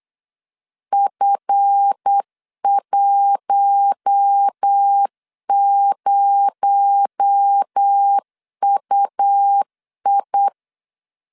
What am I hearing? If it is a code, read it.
Morse code, heard as a keyed tone, about 8 wpm, 800 Hz. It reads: F10UI
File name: LEZIONI / 8 YJ45 / 8 YJ45 8wpm 3214